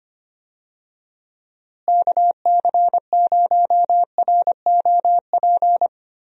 Morse code, heard as a keyed tone, about 25 wpm, 700 Hz. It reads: KC0ROP
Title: USA Call Signs - 6 Characters - Repeated 2x - 25wpm